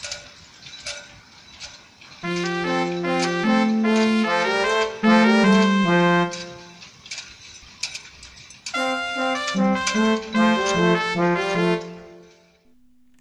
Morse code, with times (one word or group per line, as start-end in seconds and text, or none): none